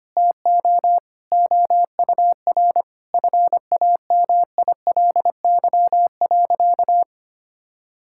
Morse code, heard as a keyed tone, about 25 wpm, 700 Hz. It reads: TO OUR FAMILY.
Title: Sentences from Top 1000 Words - Review Part 6 - 25wpm